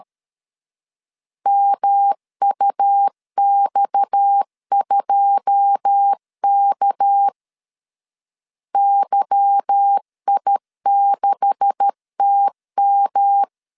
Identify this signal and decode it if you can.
Morse code, heard as a keyed tone, about 13 wpm, 800 Hz. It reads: MUX2K YI6TM
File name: CW / DET / 1 9 1500C S C 12wpm 3811